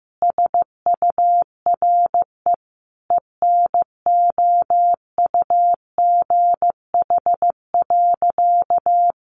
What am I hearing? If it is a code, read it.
Morse code, heard as a keyed tone, about 15 wpm, 700 Hz. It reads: SURE ENOUGH.
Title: Sentences from Top 1400 Words 15wpm